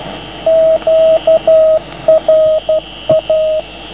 Morse code, heard as a keyed tone, about 12 wpm, 630 Hz, with radio noise. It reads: QRA